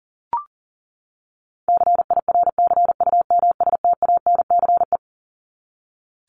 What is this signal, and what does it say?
Morse code, heard as a keyed tone, about 40 wpm, 700 Hz. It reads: CIRCUMSTANCE
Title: ES:S Suffix - Top 100 Words with Base Words 40wpm